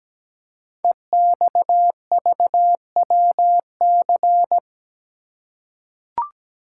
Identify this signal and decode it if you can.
Morse code, heard as a keyed tone, about 17 wpm, 700 Hz. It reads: EXVWC
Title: Code Groups - 5 Letters - 17wpm